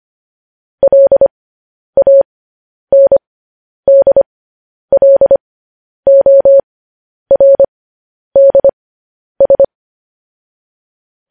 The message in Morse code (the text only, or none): LANDLORDS